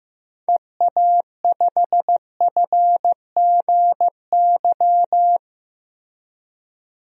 Morse code, heard as a keyed tone, about 15 wpm, 700 Hz. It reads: EA5FGY
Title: DXCC-100 Call Signs - 6 Characters - Repeated 2x - 15wpm